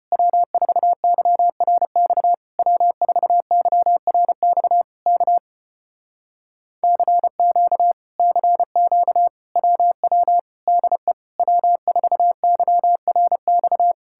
Morse code, heard as a keyed tone, about 33 wpm, 700 Hz. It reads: W4YRX W4YRX K CQ CQ WW DE W4YRX